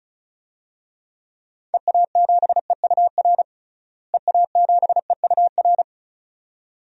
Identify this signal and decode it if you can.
Morse code, heard as a keyed tone, about 34 wpm, 700 Hz. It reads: EA7EUR EA7EUR
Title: DXCC-100 Call Signs - 6 Characters - Repeated 2x - 35wpm